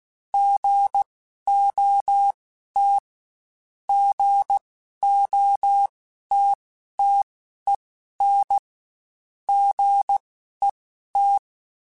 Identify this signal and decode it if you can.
Morse code, heard as a keyed tone, about 16 wpm, spaced as for 11 wpm, 800 Hz. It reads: GOT GOTTEN GET
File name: Short Runs / Short Runs-mp3s / Gotten @15